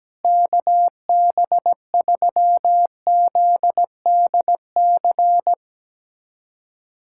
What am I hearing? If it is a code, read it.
Morse code, heard as a keyed tone, about 17 wpm, 700 Hz. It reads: KB3ZDC
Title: USA Callsign DE Callsign - 6 Characters - 17wpm